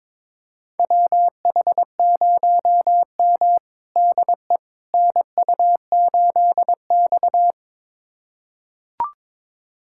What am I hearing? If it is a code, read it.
Morse code, heard as a keyed tone, about 21 wpm, 700 Hz. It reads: WH0M DE NU8X
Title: USA Callsign DE Callsign - 4 Characters - 22wpm